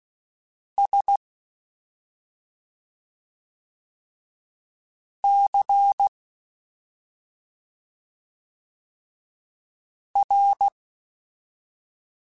Morse code, heard as a keyed tone, about 16 wpm, spaced as for 2 wpm, 800 Hz. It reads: SCR
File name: Special INGs-mp3s / Special INGs-mp3s / Slow ing @10